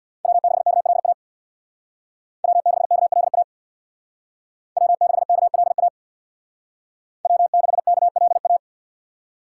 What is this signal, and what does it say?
Morse code, heard as a keyed tone, about 64 wpm, 700 Hz. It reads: W6KLA W6KLA W6KLA W6KLA